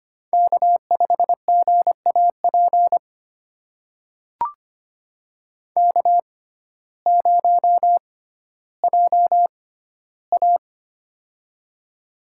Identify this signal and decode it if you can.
Morse code, heard as a keyed tone, about 25 wpm, spaced as for 14 wpm, 700 Hz. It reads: K5GAP K0JA
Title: USA Call Signs - 6x Character Spacing - 25wpm